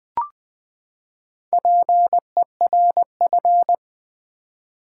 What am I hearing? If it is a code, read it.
Morse code, heard as a keyed tone, about 20 wpm, 700 Hz. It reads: PERF
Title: Build A Word - 4 Letters At A Time - 11-Letter Words - 20wpm